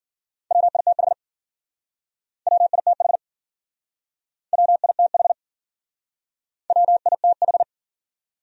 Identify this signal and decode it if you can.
Morse code, heard as a keyed tone, about 48 wpm, 700 Hz. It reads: WITH WITH WITH WITH
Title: Speed-Racing Top 100 Words 40wpm